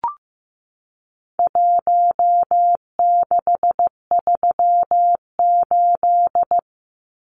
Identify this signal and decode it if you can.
Morse code, heard as a keyed tone, about 15 wpm, 700 Hz. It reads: 1638
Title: Numbers - Contest Style - 4-Digit 15wpm